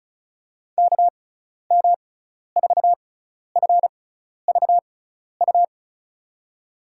Morse code, heard as a keyed tone, about 36 wpm, spaced as for 12 wpm, 700 Hz. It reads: KM4FVU
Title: USA Call Signs - 6 Characters - 6x Character Spacing - 35wpm